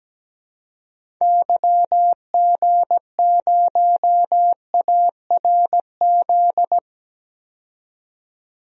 Morse code, heard as a keyed tone, about 17 wpm, 700 Hz. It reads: YG0ARZ
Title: DXCC-100 Call Signs - 6 Characters - 17wpm